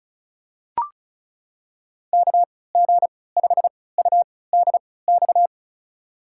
Morse code, heard as a keyed tone, about 34 wpm, spaced as for 20 wpm, 700 Hz. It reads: KG5UDX
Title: USA Call Signs - 6 Characters - 3x Character Spacing - 35wpm